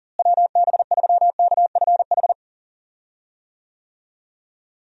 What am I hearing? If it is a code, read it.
Morse code, heard as a keyed tone, about 40 wpm, 700 Hz. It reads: WB3KFH
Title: USA Call Signs - 6 Characters - Repeated 2x - 40wpm